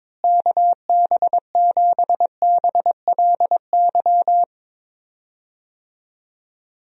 Morse code, heard as a keyed tone, about 21 wpm, 700 Hz. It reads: KB7BLY
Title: USA Call Signs - 6 Characters - Repeated 3x - 22wpm